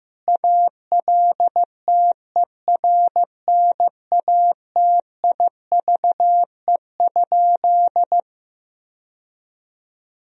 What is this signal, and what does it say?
Morse code, heard as a keyed tone, about 15 wpm, 700 Hz. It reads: ALTERNATIVE?
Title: Sentences from Top 1000 Words - Review Part 4 - 15wpm